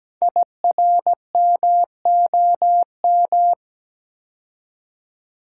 Morse code, heard as a keyed tone, about 17 wpm, 700 Hz. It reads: IRMOM
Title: Code Groups - 5 Letters - 17wpm